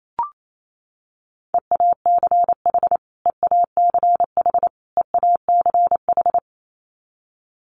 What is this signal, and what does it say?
Morse code, heard as a keyed tone, about 28 wpm, 700 Hz. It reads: EACH EACH EACH
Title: Top 500 Words Repeated 28wpm